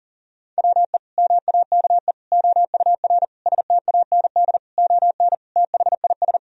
Morse code, heard as a keyed tone, about 40 wpm, 700 Hz. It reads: WE MAKE OUR STAND ON THIS
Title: Sentences from Top 1000 Words - Review Part 2 - 40wpm